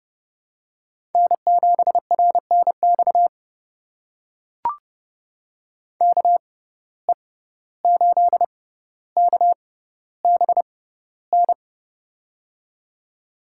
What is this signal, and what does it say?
Morse code, heard as a keyed tone, about 30 wpm, spaced as for 15 wpm, 700 Hz. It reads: N7RNX KE8KBN
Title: USA Call Signs - 6x Character Spacing - 30wpm